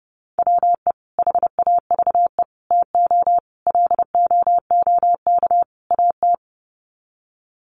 Morse code, heard as a keyed tone, about 30 wpm, 700 Hz. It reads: WE HAVE TO LOOK AT
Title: Sets of 5 Words Encore2 30wpm